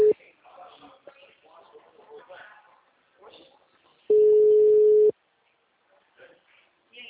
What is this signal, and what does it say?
Ringback tone (Europe, 425 Hz), with speech.